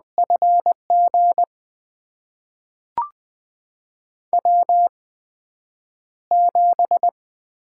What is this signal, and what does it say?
Morse code, heard as a keyed tone, about 20 wpm, spaced as for 9 wpm, 700 Hz. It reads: FG W7
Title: USA Call Signs - 5 Characters - 8x Character Spacing - 20wpm